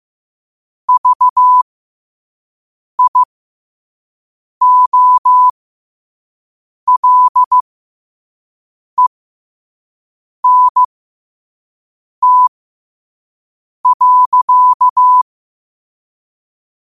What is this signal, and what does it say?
Morse code, heard as a keyed tone, about 15 wpm, spaced as for 5 wpm, 1000 Hz. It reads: VIOLENT.
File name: qotdfiles / 5wpm